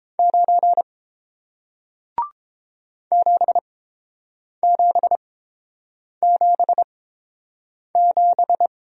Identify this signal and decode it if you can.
Morse code, heard as a keyed tone, about 30 wpm, spaced as for 7 wpm, 700 Hz. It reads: 9 7777